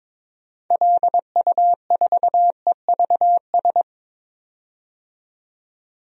Morse code, heard as a keyed tone, about 22 wpm, 700 Hz. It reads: LU4EVS
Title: DXCC-100 Call Signs - 6 Characters - 22wpm